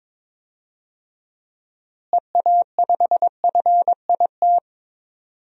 Morse code, heard as a keyed tone, about 22 wpm, 700 Hz. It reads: EA5FIT
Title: DXCC-100 Call Signs - 6 Characters - Repeated 3x - 22wpm